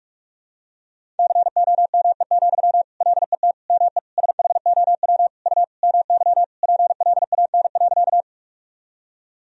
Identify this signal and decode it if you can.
Morse code, heard as a keyed tone, about 46 wpm, 700 Hz. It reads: COME, LET ME SHOW U MY PLAN.
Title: Sentences from Top 500 Words and Abbreviations 45wpm